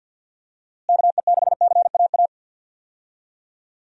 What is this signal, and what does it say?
Morse code, heard as a keyed tone, about 52 wpm, 700 Hz. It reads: KE6CAA
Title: USA Call Signs - 6 Characters - Repeated 3x - 50wpm